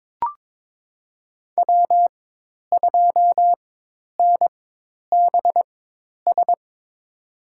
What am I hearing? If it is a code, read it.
Morse code, heard as a keyed tone, about 22 wpm, spaced as for 10 wpm, 700 Hz. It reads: W2NBS
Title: USA Call Signs - 5 Characters - 4x Character Spacing - 22wpm